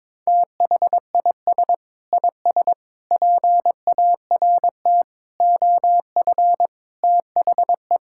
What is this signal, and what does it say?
Morse code, heard as a keyed tone, about 22 wpm, 700 Hz. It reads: THIS IS PART OF THE
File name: practice / 22wpm / Sentences with 2-Word Phrases - 22wpm